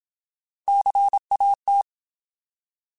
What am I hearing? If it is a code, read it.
Morse code, heard as a keyed tone, about 27 wpm, 800 Hz. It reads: CAT
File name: Alphabetical